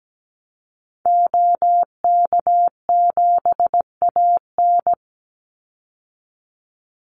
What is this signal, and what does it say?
Morse code, heard as a keyed tone, about 17 wpm, 700 Hz. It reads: OK7AN
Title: Call Signs Repeated 3x - 17wpm